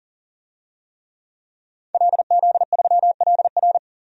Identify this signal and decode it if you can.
Morse code, heard as a keyed tone, about 40 wpm, 700 Hz. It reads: LZ3LR
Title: DXCC-100 Call Signs - 5 Characters - 40wpm